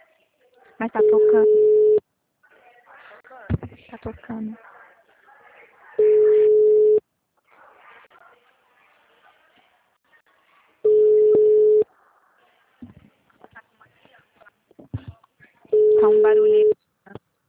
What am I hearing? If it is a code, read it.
Ringback tone (Europe, 425 Hz), with speech.